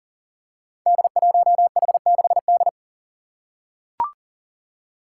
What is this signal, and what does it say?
Morse code, heard as a keyed tone, about 42 wpm, 700 Hz. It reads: D1H6D
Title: Five Challenging Letter-Number Combinations 40wpm